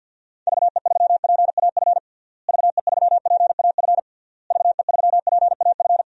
Morse code, heard as a keyed tone, about 52 wpm, 700 Hz. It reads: VE3PAF VE3PAF VE3PAF
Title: DXCC-100 Call Signs - 6 Characters - Repeated 3x - 50wpm